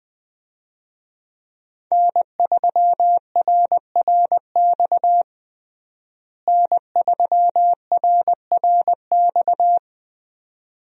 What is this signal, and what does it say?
Morse code, heard as a keyed tone, about 20 wpm, 700 Hz. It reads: N3RRX N3RRX
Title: USA Call Signs - 5 Characters - Repeated 2x - 20wpm